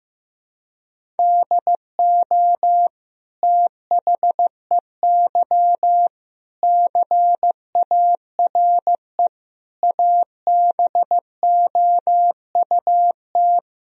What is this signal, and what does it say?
Morse code, heard as a keyed tone, about 15 wpm, 700 Hz. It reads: DO THEY CARE ABOUT
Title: Sentences from Top 1000 Words - Review Part 3 - 15wpm